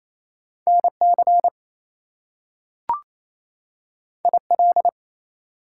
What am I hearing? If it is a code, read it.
Morse code, heard as a keyed tone, about 28 wpm, 700 Hz. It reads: NC IL
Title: US States and Canadian Provinces 28wpm